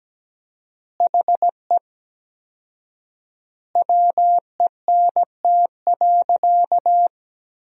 Morse code, heard as a keyed tone, about 17 wpm, 700 Hz. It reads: HE WENT.
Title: 4x Word-Spacing - Sentences from Top 200 Words - 17wpm